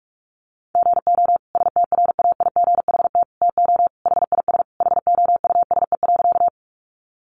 Morse code, heard as a keyed tone, about 44 wpm, 700 Hz. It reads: GO STRAIGHT TO HIS HOUSE.
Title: Sentences from Top 1000 Words 45wpm